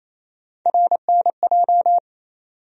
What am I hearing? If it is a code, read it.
Morse code, heard as a keyed tone, about 28 wpm, 700 Hz. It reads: RNJ
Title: USA Call Sign Postfixes - 3 Letters 28wpm